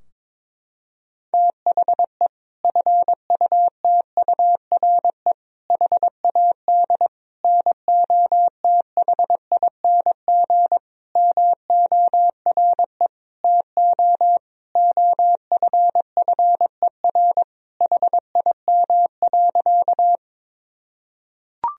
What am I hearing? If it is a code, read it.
Morse code, heard as a keyed tone, about 22 wpm, 700 Hz. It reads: THE FUTURE HAD NOTHING MORE TO OFFER HIM.
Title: Sentences from Top 1000 Words - Review Part 4 - 22wpm